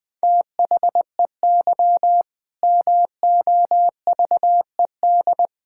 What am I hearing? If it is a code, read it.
Morse code, heard as a keyed tone, about 21 wpm, 700 Hz. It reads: THEY MOVED
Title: Sentences with 4-Word Phrases - 20wpm